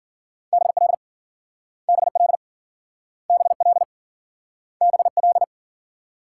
Morse code, heard as a keyed tone, about 52 wpm, 700 Hz. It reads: BL BL BL BL